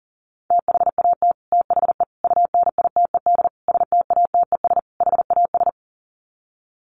Morse code, heard as a keyed tone, about 40 wpm, 700 Hz. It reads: THAT THE UNITED STATES HAS
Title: Sets of 5 Words Encore 40wpm